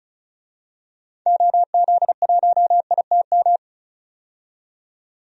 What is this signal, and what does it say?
Morse code, heard as a keyed tone, about 34 wpm, 700 Hz. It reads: OZ1ITM
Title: DXCC-100 Call Signs - 6 Characters - 35wpm